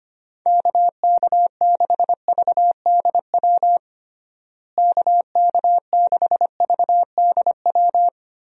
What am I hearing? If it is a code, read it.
Morse code, heard as a keyed tone, about 25 wpm, 700 Hz. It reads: KK6VDW KK6VDW
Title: USA Call Signs - 6 Characters - Repeated 3x - 25wpm